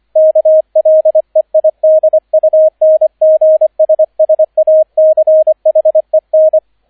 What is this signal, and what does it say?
Morse code, heard as a keyed tone, about 25 wpm, 630 Hz, with radio noise. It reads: KLEIDUNGSSACHEN